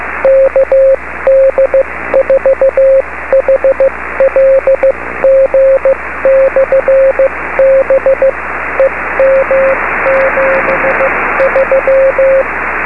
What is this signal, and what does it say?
Morse code, heard as a keyed tone, about 15 wpm, spaced as for 13 wpm, 540 Hz, with radio noise. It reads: KD4HLG/B EM73
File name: beacons / kd4hlg-b